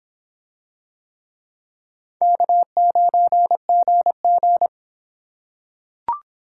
Morse code, heard as a keyed tone, about 24 wpm, 700 Hz. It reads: K9GG